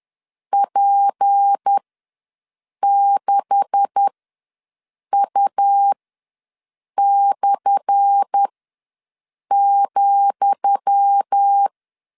Morse code, heard as a keyed tone, about 10 wpm, spaced as for 6 wpm, 800 Hz. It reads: P6U/,
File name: LEZIONI / 10 SIGNS / 10D COMA ws 10wpm 4314